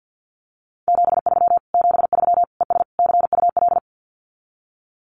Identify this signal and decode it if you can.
Morse code, heard as a keyed tone, about 50 wpm, 700 Hz. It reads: 73 73 ES CUL
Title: Common QSO Phrases Encore 50wpm